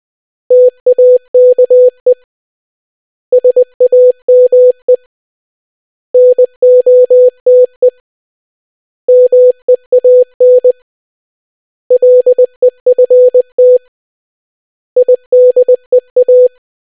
Morse code, heard as a keyed tone, about 20 wpm, 500 Hz. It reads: TAKE SAME NOTE MEAN LEFT IDEA